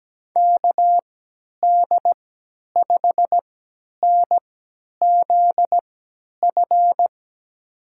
Morse code, heard as a keyed tone, about 17 wpm, spaced as for 10 wpm, 700 Hz. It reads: KD5NZF
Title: USA Call Signs - 6 Characters - 3x Character Spacing - 17wpm